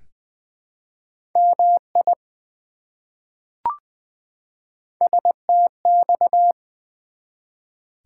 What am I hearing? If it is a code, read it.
Morse code, heard as a keyed tone, about 20 wpm, 700 Hz. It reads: MI STX
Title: ARRL Sections v2 - 20wpm